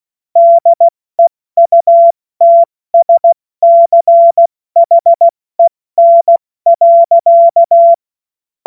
Morse code, heard as a keyed tone, about 16 wpm, 680 Hz. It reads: DEUTSCHEN.
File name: Morsekurs / Witze / Witz 07.November 2023 70 F80